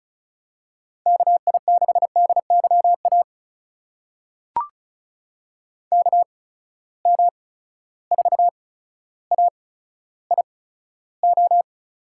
Morse code, heard as a keyed tone, about 36 wpm, spaced as for 15 wpm, 700 Hz. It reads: KI6DYA KM4AIO
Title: USA Call Signs - 6 Characters - 8x Character Spacing - 35wpm